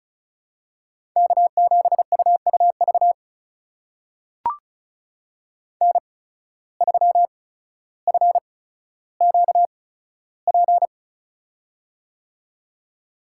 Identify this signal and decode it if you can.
Morse code, heard as a keyed tone, about 36 wpm, spaced as for 15 wpm, 700 Hz. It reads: K7UUV N3FQP
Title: USA Call Signs - 5 Characters - 8x Character Spacing - 35wpm